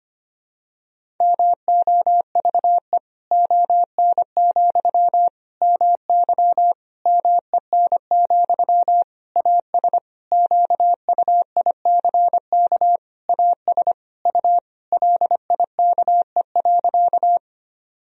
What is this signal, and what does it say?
Morse code, heard as a keyed tone, about 25 wpm, 700 Hz. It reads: MOVE ON, MY MEN, AS QUICK AS U LIKE.
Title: Sentences from Top 500 Words and Abbreviations 25wpm